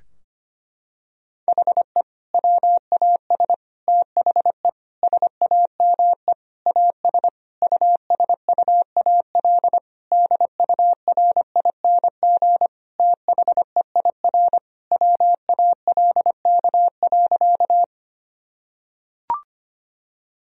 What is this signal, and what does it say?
Morse code, heard as a keyed tone, about 25 wpm, 700 Hz. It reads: HE WAS THE SAME AS USUAL DURING THEIR WALK.
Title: Sentences from Top 1000 Words - Review Part 2 - 25wpm